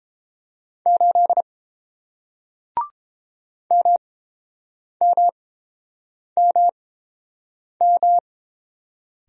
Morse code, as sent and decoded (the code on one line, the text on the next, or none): ---.. / -- -- -- --
8 MMMM